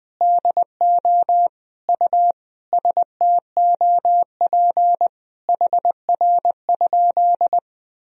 Morse code, heard as a keyed tone, about 20 wpm, 700 Hz. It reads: DO U STOP HR?
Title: Sentences from Top 300 Words and Abbreviations 20wpm